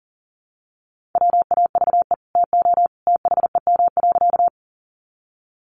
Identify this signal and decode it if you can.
Morse code, heard as a keyed tone, about 40 wpm, 700 Hz. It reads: WAVE TO THEM.
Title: Sentences from Top 1400 Words 40wpm